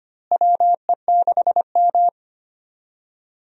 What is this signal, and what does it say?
Morse code, heard as a keyed tone, about 25 wpm, 700 Hz. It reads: WE6M
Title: USA Call Signs - 4 Characters - Repeated 3x - 25wpm